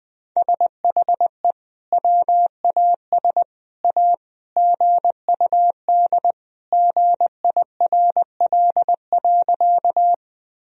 Morse code, heard as a keyed tone, about 20 wpm, 700 Hz. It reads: SHE WAS A GUD GIRL.
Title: Sentences from Top 400 Words and Abbreviations 20wpm